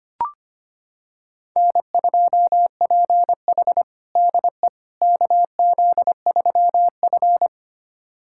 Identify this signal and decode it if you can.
Morse code, heard as a keyed tone, about 25 wpm, 700 Hz. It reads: N2PH DE KZ3F
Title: USA Callsign DE Callsign - 4 Characters - 25wpm